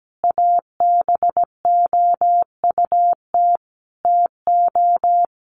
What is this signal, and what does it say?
Morse code, heard as a keyed tone, about 17 wpm, 700 Hz. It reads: ABOUT TO
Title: Sentences from Top 1200 Words 17wpm